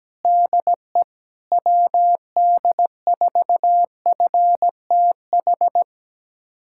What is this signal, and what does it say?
Morse code, heard as a keyed tone, about 17 wpm, 700 Hz. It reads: DE WD4FTH
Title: USA Callsign DE Callsign - 6 Characters - 17wpm